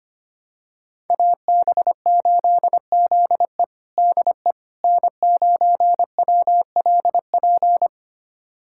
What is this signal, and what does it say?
Morse code, heard as a keyed tone, about 25 wpm, 700 Hz. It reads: AB8ZE DE N9WLP
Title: USA Callsign DE Callsign - 5 Characters - 25wpm